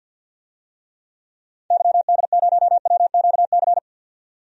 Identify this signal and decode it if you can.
Morse code, heard as a keyed tone, about 48 wpm, 700 Hz. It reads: YD0WQC
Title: DXCC-100 Call Signs - 6 Characters - 50wpm